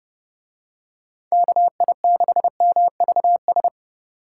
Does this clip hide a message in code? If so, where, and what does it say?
Yes: Morse code from 1.3 to 3.7 s, heard as a keyed tone, about 27 wpm, 700 Hz. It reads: KI6MVS